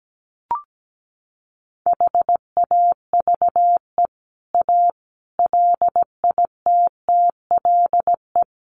Morse code, heard as a keyed tone, about 17 wpm, 700 Hz. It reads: HAVE A LITTLE
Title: Sentences from Top 700 Words 17wpm